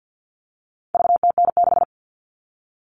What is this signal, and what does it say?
Morse code, heard as a keyed tone, about 50 wpm, 700 Hz. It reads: VTN6
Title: All Cut Numbers - 4-Digit 50wpm